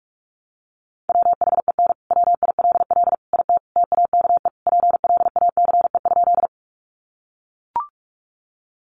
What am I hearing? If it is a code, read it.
Morse code, heard as a keyed tone, about 44 wpm, 700 Hz. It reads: WHEN WILL IT TAKE PLACE?
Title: Sentences from Top 200 Words 45wpm